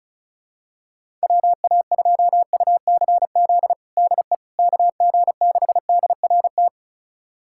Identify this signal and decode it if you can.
Morse code, heard as a keyed tone, about 34 wpm, 700 Hz. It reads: WA2UCZ DE KG6DRT
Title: USA Callsign DE Callsign - 6 Characters - 35wpm